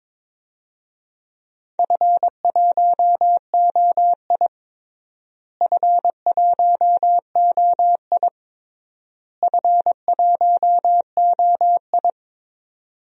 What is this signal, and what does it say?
Morse code, heard as a keyed tone, about 22 wpm, 700 Hz. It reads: F1OI F1OI F1OI
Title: DXCC-100 Call Signs - 4 Characters - Repeated 3x - 22wpm